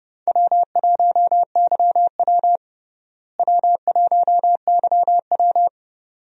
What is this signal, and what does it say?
Morse code, heard as a keyed tone, about 29 wpm, 700 Hz. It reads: W1YW W1YW
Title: USA Call Signs - 4 Characters - Repeated 2x - 30wpm